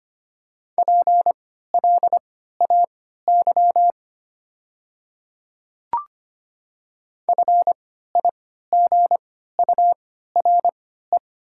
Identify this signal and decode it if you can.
Morse code, heard as a keyed tone, about 25 wpm, spaced as for 14 wpm, 690 Hz. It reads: PLAY FIGURE
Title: 3x Character Spacing - Top 500 Words - 25wpm